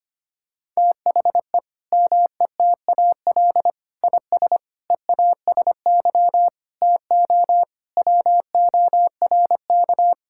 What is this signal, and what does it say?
Morse code, heard as a keyed tone, about 25 wpm, 700 Hz. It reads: THE METAL IS EASY TO WORK
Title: Sentences from Top 1000 Words - Review Part 6 - 25wpm